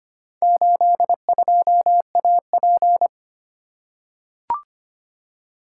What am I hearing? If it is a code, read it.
Morse code, heard as a keyed tone, about 25 wpm, 700 Hz. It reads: 82AP
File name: practice / 25wpm / Four Letter-Number Combinations - 25wpm